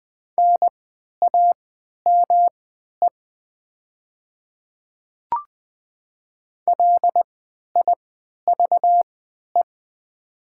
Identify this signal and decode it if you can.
Morse code, heard as a keyed tone, about 20 wpm, spaced as for 11 wpm, 700 Hz. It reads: NAME LIVE
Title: 3x Character Spacing - Top 200 Words - 20wpm